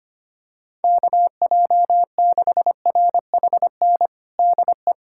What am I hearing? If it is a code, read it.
Morse code, heard as a keyed tone, about 25 wpm, 700 Hz. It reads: KJ6RHN DE
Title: USA Callsign DE Callsign 25wpm